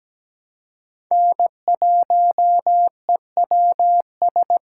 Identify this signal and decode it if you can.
Morse code, heard as a keyed tone, about 17 wpm, 700 Hz. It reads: N1EWS
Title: USA Call Signs - 5 Characters - 4x Character Spacing - 17wpm